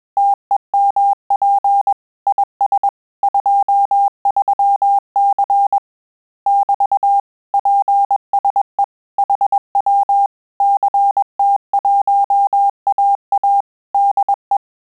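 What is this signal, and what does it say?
Morse code, heard as a keyed tone, about 21 wpm, 800 Hz. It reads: TEMP IS 23C <BT> PSE HW CT1AA DE
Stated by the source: Morse QSO practice at 20 words per minute, in Italian and English.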